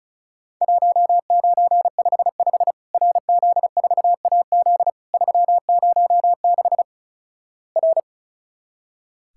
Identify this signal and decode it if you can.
Morse code, heard as a keyed tone, about 36 wpm, 700 Hz. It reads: 1955 RZ4AZ 306 R
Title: Worked All Europe DX Contest - QTC Exchange - 35wpm